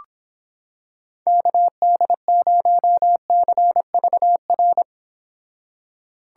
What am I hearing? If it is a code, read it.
Morse code, heard as a keyed tone, about 26 wpm, 700 Hz. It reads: KD0CVR